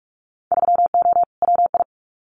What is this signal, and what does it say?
Morse code, heard as a keyed tone, about 44 wpm, 700 Hz. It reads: 3O WI
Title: Winter Field Day Exchanges 45wpm